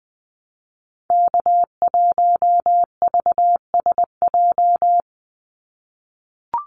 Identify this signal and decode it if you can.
Morse code, heard as a keyed tone, about 20 wpm, 700 Hz. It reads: K1VSJ
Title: Call Signs - 20wpm